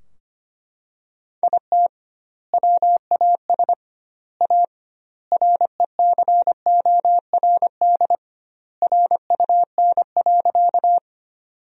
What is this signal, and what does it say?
Morse code, heard as a keyed tone, about 25 wpm, 700 Hz. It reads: IT WAS A RECORD RUN.
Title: 2x Word-Spacing - Sentences from Top 500 Words - 25wpm